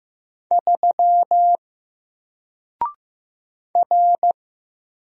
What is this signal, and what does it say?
Morse code, heard as a keyed tone, about 15 wpm, 700 Hz. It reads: 3 R
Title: Single Character (Letter:Number:Punctuation:Prosign) 15wpm